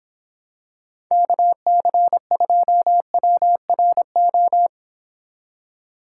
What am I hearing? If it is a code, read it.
Morse code, heard as a keyed tone, about 26 wpm, 700 Hz. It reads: KC2WRO